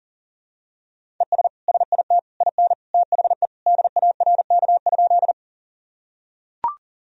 Morse code, heard as a keyed tone, about 40 wpm, 700 Hz. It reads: ES SIT IN THE DARK?
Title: Sentences from Top 500 Words and Abbreviations 40wpm